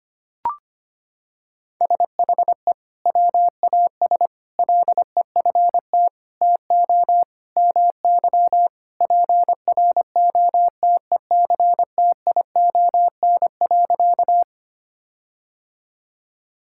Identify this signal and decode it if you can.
Morse code, heard as a keyed tone, about 25 wpm, 700 Hz. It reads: SHE WAS LEFT TO MY PROTECTION.
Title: Sentences from Top 1600 Words 25wpm